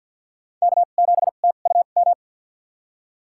Morse code, heard as a keyed tone, about 47 wpm, spaced as for 35 wpm, 700 Hz. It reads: K7TUM
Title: USA Call Signs - 5 Characters - 2x Character Spacing - 50wpm